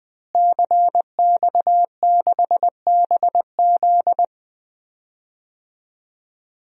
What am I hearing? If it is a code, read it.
Morse code, heard as a keyed tone, about 20 wpm, 700 Hz. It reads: CX6BZ
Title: DXCC-100 Call Signs - 5 Characters - Repeated 2x - 20wpm